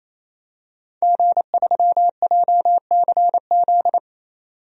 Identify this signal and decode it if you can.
Morse code, heard as a keyed tone, about 28 wpm, 700 Hz. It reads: G3JCZ
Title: DXCC-100 Call Signs - 5 Characters - 28wpm